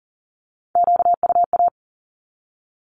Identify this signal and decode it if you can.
Morse code, heard as a keyed tone, about 41 wpm, 700 Hz. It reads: QUA
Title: Q-Codes with Spoken Meaning - 40wpm